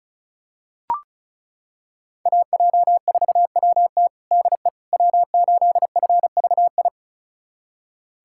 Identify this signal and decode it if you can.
Morse code, heard as a keyed tone, about 36 wpm, 700 Hz. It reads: AJ4WT DE W8FVI
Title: USA Callsign DE Callsign - 5 Characters - 35wpm